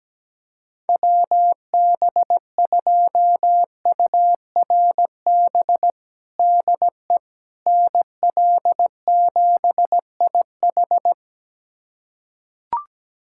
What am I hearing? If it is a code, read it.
Morse code, heard as a keyed tone, about 17 wpm, 700 Hz. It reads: WB2URB DE NL7IH
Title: USA Callsign DE Callsign 17wpm